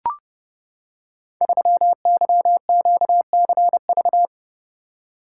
Morse code, heard as a keyed tone, about 30 wpm, 700 Hz. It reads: 3YQCV
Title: Code Groups - 5 Letter:Numbers - 30wpm